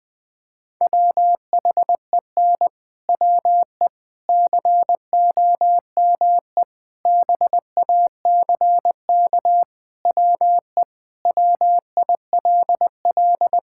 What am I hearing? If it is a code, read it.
Morse code, heard as a keyed tone, about 20 wpm, 700 Hz. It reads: WHEN WE COME BACK WE WILL
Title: Sentences with 4-Word Phrases - Spoken Beginning - 20wpm